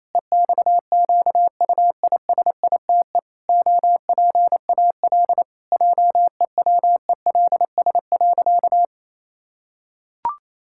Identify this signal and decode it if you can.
Morse code, heard as a keyed tone, about 28 wpm, 700 Hz. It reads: EXQUISITE OPAL JEWELS.
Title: Pangram Challenge v2 - 28wpm